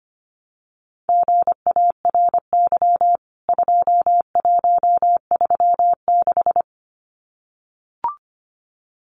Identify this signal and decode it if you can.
Morse code, heard as a keyed tone, about 25 wpm, 700 Hz. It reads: GARY 2136
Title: CWT Exchange 25wpm